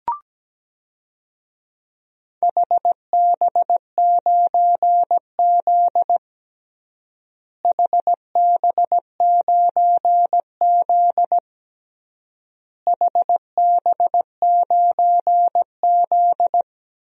Morse code, heard as a keyed tone, about 17 wpm, 700 Hz. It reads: HB9Z HB9Z HB9Z
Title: DXCC-100 Call Signs - 4 Characters - Repeated 3x - 17wpm